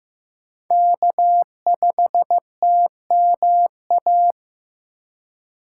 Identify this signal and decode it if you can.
Morse code, heard as a keyed tone, about 15 wpm, 700 Hz. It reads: K5TMA
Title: USA Call Signs - 5 Characters - 4x Character Spacing - 15wpm